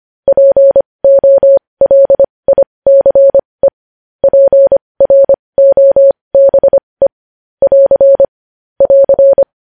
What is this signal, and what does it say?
Morse code, heard as a keyed tone, about 25 wpm, spaced as for 20 wpm, 550 Hz. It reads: POLICE PROBE <AR> <AR>
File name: podcast / News headlines-1x-25x20